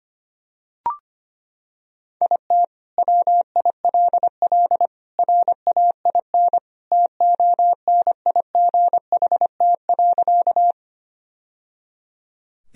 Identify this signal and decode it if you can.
Morse code, heard as a keyed tone, about 25 wpm, 700 Hz. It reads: IT WILL RAIN TONIGHT.
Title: Sentences from Top 1000 Words - Review Part 1 - 25wpm